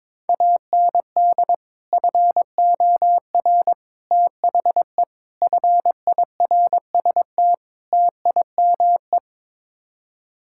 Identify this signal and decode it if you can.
Morse code, heard as a keyed tone, about 22 wpm, 700 Hz. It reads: AND FOR THE FIRST TIME
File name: practice / 22wpm / Build A Phrase - 5-Word Phrases - 22wpm